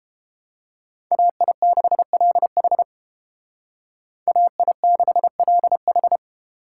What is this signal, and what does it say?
Morse code, heard as a keyed tone, about 31 wpm, 700 Hz. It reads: AI6LH AI6LH